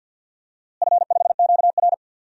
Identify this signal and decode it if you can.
Morse code, heard as a keyed tone, about 52 wpm, 700 Hz. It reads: F5QR
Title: DXCC-100 Call Signs - 4 Characters - Repeated 2x - 50wpm